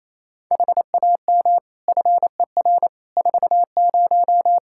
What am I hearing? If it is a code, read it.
Morse code, heard as a keyed tone, about 28 wpm, 700 Hz. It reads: HAM FER 40
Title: Practice QSO - Exchange 3 - More Info - 28wpm